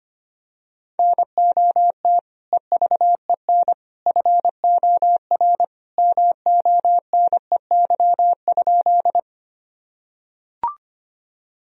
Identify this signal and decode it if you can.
Morse code, heard as a keyed tone, about 25 wpm, 700 Hz. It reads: NOT EVEN FOR MONEY?
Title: Sentences from Top 1000 Words - Review Part 1 - 25wpm